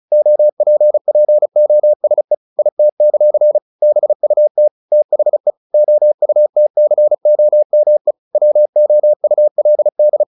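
Morse code, heard as a keyed tone, about 34 wpm, 600 Hz. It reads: OPPOSE IT; BUT THE OUTCOME WOULD
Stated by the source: Morse practice at 35 wpm from CW med Gnister, sending Ken Follett's novel Never in English.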